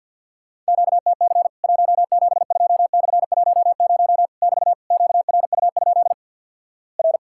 Morse code, heard as a keyed tone, about 50 wpm, 700 Hz. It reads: QTC 172/10 <BT> QRU? R
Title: Worked All Europe DX Contest - QTC Exchange - 50wpm